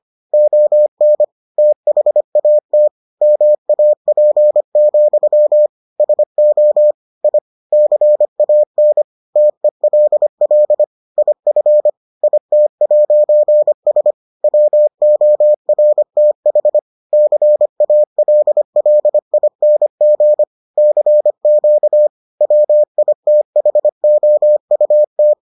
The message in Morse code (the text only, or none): ON THAT MAP, SO I CAN TELL IF IT'S WORTH CALLING CQ WITHOUT